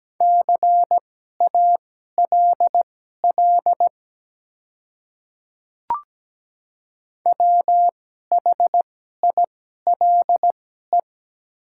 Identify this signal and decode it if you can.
Morse code, heard as a keyed tone, about 17 wpm, spaced as for 12 wpm, 700 Hz. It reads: CALL WHILE
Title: 2x Character Spacing - Top 300 Words - 17wpm